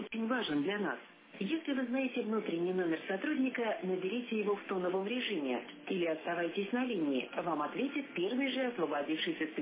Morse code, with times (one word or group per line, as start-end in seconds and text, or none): none